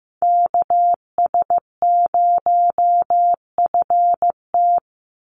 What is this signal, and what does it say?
Morse code, heard as a keyed tone, about 15 wpm, 700 Hz. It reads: KS0FT
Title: USA Call Signs 15wpm